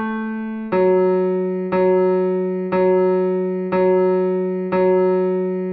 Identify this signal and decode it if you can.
There is no code.